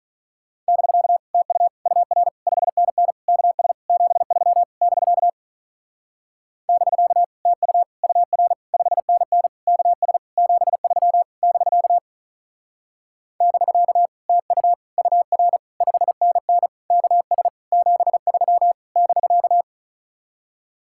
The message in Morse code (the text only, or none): <BK> TU UR 5NN KS 73 <BK> <BK> TU UR 5NN KS 73 <BK> <BK> TU UR 5NN KS 73 <BK>